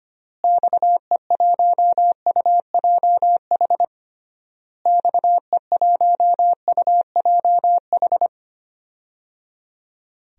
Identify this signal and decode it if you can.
Morse code, heard as a keyed tone, about 25 wpm, 700 Hz. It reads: XE1UJH XE1UJH
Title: DXCC-100 Call Signs - 6 Characters - Repeated 3x - 25wpm